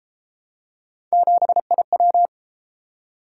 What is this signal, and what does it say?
Morse code, heard as a keyed tone, about 33 wpm, 700 Hz. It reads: 7IW